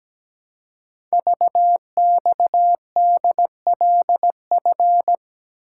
Morse code, heard as a keyed tone, about 17 wpm, 700 Hz. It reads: VXDLF
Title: Code Groups - 5 Letters - 17wpm